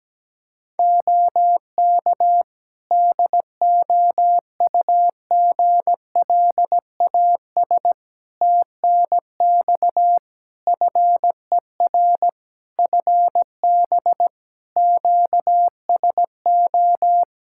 Morse code, heard as a keyed tone, about 17 wpm, 700 Hz. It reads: OK DOUGLAS TNX FER FB QSO
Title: Practice QSO - Exchange 4 - Ending - 17wpm